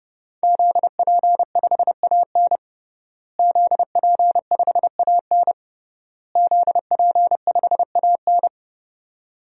Morse code, heard as a keyed tone, about 30 wpm, 700 Hz. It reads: ZP5AN ZP5AN ZP5AN
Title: DXCC-100 Call Signs - 5 Characters - Repeated 3x - 30wpm